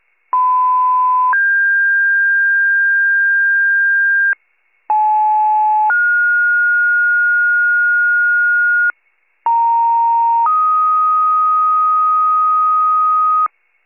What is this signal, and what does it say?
Two-tone page, heard as a one-second tone, then a three-second tone, about 1010 then 1600 Hz, about 860 then 1430 Hz, about 930 then 1250 Hz, with noise. It is heard from 0.3 to 13.5 s.
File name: wcalert